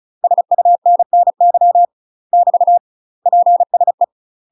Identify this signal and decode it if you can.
Morse code, heard as a keyed tone, about 34 wpm, 700 Hz. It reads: SUNNY <BT> PSE
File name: UK MASTER QSOs TRAINING / 35WPM / UK MASTER QSOs 0024